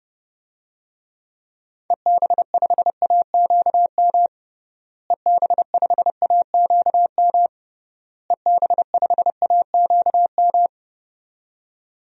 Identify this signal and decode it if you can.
Morse code, heard as a keyed tone, about 30 wpm, 700 Hz. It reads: EB5AQM EB5AQM EB5AQM
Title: DXCC-100 Call Signs - 6 Characters - Repeated 3x - 30wpm